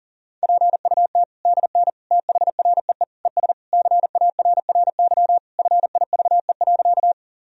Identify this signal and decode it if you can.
Morse code, heard as a keyed tone, about 40 wpm, 700 Hz. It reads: PUT DN THREE ES CARRY FIVE.
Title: Sentences from Top 400 Words and Abbreviations 40wpm